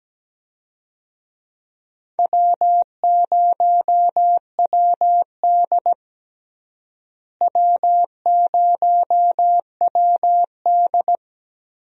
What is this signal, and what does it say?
Morse code, heard as a keyed tone, about 17 wpm, 700 Hz. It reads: W0WD W0WD
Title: USA Call Signs - 4 Characters - Repeated 2x - 17wpm